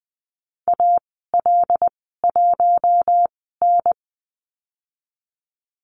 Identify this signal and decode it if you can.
Morse code, heard as a keyed tone, about 20 wpm, spaced as for 14 wpm, 700 Hz. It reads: AL1N
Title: 082 Lesson 14g - Number 2 - Call Signs 20wpm